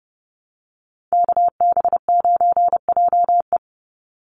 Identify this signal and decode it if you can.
Morse code, heard as a keyed tone, about 31 wpm, 700 Hz. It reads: KB9JE
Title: USA Call Signs 30wpm